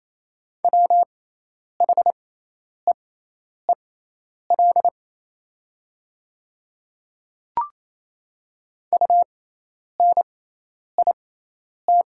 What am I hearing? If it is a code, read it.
Morse code, heard as a keyed tone, about 28 wpm, spaced as for 10 wpm, 700 Hz. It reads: WHEEL UNIT
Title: 6x Character Spacing - Top 500 Words - 28wpm